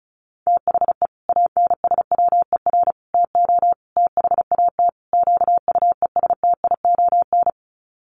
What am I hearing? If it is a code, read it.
Morse code, heard as a keyed tone, about 34 wpm, 700 Hz. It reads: THE ANSWER TO THAT QUESTION
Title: Sets of 5 Words Encore2 35wpm